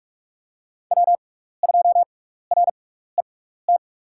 Morse code, heard as a keyed tone, about 46 wpm, spaced as for 16 wpm, 700 Hz. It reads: W2RET
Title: USA Call Signs - 6x Character Spacing - 45wpm